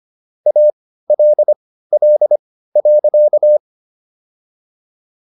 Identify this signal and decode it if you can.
Morse code, heard as a keyed tone, about 25 wpm, spaced as for 15 wpm, 600 Hz. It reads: ALL.
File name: SS119 15